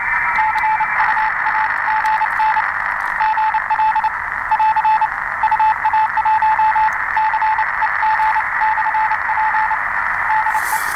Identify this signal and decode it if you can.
Morse code, heard as a keyed tone, about 29 wpm, 870 Hz, with radio noise. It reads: CUAGN GL <AR> UA1CEG/P DE